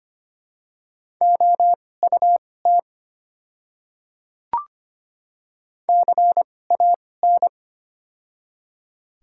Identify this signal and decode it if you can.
Morse code, heard as a keyed tone, about 25 wpm, spaced as for 18 wpm, 700 Hz. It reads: OUT CAN
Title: 2x Character Spacing - Top 100 Words - 25wpm